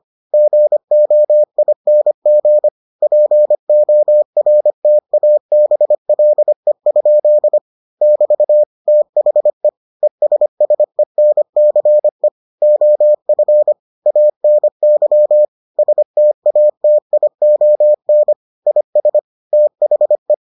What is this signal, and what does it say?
Morse code, heard as a keyed tone, about 25 wpm, 600 Hz. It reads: GOING PORTABLE? <BT> THE ESSENCE OF ANY STATION IS THE